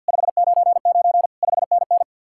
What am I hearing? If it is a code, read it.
Morse code, heard as a keyed tone, about 59 wpm, 700 Hz. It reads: 599 5NN
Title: Speed-Racing Common QSO Phrases Encore - 50wpm